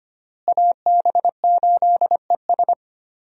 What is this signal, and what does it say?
Morse code, heard as a keyed tone, about 25 wpm, 700 Hz. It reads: AB8ES
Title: USA Call Signs - 5 Characters - Repeated 2x - 25wpm